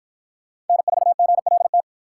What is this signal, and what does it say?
Morse code, heard as a keyed tone, about 48 wpm, 700 Hz. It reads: N4GLT